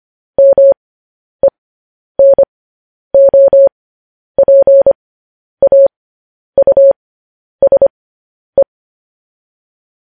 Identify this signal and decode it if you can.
Morse code, heard as a keyed tone, about 25 wpm, spaced as for 10 wpm, 550 Hz. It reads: MENOPAUSE
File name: podcast / News headlines-1x-25x10